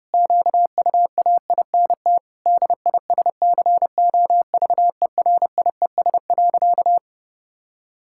Morse code, heard as a keyed tone, about 30 wpm, 700 Hz. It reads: QUAINT DISCOVERIES.
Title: Pangram Challenge v2 - 30wpm